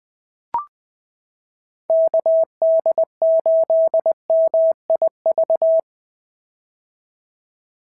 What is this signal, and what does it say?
Morse code, heard as a keyed tone, about 20 wpm, 690 Hz. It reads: KD8MIV
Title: POTA QSO Practice - 20wpm